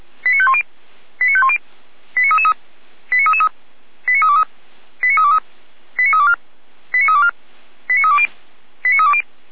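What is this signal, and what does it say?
Five-tone selective call (ZVEI), heard as quick runs of five tones, tones of about 70 ms. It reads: 89510 89510 89333 89333 89323 89323 89324 89324 89320 89320
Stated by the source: Five-tone ZVEI selective call received by SDR, fire brigade Feuerwehr Orlamünde.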